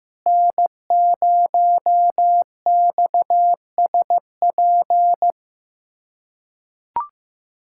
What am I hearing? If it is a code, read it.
Morse code, heard as a keyed tone, about 15 wpm, 700 Hz. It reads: N0XSP
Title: USA Callsign DE Callsign - 5 Characters - 15wpm